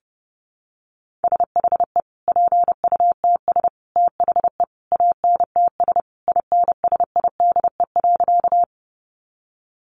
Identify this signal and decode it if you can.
Morse code, heard as a keyed tone, about 30 wpm, 700 Hz. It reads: SHE PUTS THE ANTS INSIDE.
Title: Easy Phrases 30wpm